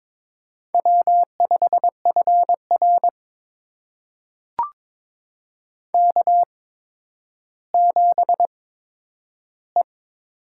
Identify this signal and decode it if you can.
Morse code, heard as a keyed tone, about 22 wpm, spaced as for 11 wpm, 700 Hz. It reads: W5FR K7E
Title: USA Call Signs - 8x Character Spacing - 22wpm